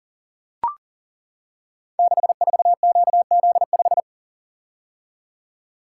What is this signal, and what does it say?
Morse code, heard as a keyed tone, about 40 wpm, 700 Hz. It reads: 64QZ5
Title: Code Groups - 5 Letter:Numbers - 40wpm